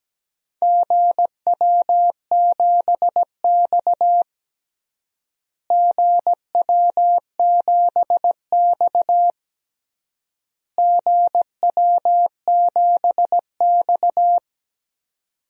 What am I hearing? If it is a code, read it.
Morse code, heard as a keyed tone, about 17 wpm, 700 Hz. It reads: GW7X GW7X GW7X
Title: DXCC-100 Call Signs - 4 Characters - Repeated 3x - 17wpm